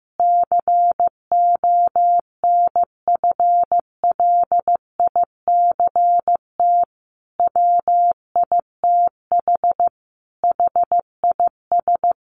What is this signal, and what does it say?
Morse code, heard as a keyed tone, about 15 wpm, 700 Hz. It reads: CONFLICT WITH HIS
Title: Sentences from Top 1300 Words 15wpm